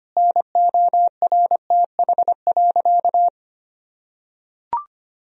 Morse code, heard as a keyed tone, about 25 wpm, 700 Hz. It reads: NORTH.
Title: Sentences from Top 1000 Words - Review Part 2 - 25wpm